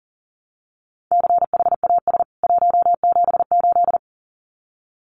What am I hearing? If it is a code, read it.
Morse code, heard as a keyed tone, about 40 wpm, 700 Hz. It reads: CHAS 178
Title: CWT Exchange 40wpm